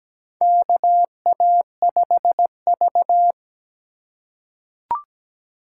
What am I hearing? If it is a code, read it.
Morse code, heard as a keyed tone, about 17 wpm, 700 Hz. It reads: KA5V
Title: USA Call Signs - 3x Character Spacing - 17wpm